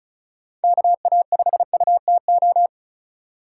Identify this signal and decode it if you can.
Morse code, heard as a keyed tone, about 34 wpm, 700 Hz. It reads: KA5UTO
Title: USA Call Signs - 3x Character Spacing - 35wpm